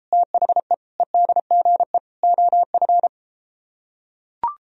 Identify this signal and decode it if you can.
Morse code, heard as a keyed tone, about 34 wpm, 700 Hz. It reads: THE EDGE OF